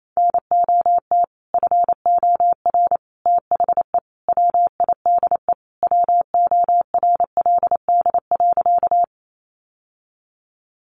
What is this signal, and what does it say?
Morse code, heard as a keyed tone, about 28 wpm, 700 Hz. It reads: NOT FOR THE WIDE WORLD.
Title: Sentences from Top 900 Words 28wpm